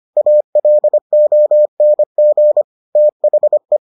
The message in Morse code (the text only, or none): ALONG THE